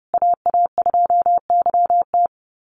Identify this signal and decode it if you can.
Morse code, heard as a keyed tone, about 28 wpm, 700 Hz. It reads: AA2YT
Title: USA Call Signs Repeated 3x 30wpm